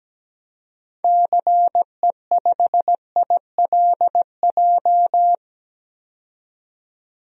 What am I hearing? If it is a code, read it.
Morse code, heard as a keyed tone, about 17 wpm, 700 Hz. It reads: CE5ILJ
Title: DXCC-100 Call Signs - 6 Characters - 17wpm